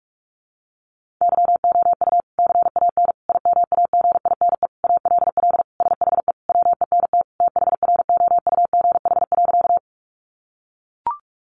Morse code, heard as a keyed tone, about 44 wpm, 700 Hz. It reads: YOU CAN IMAGINE ALL SHE WENT THROUGH.
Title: Sentences from Top 1400 Words 45wpm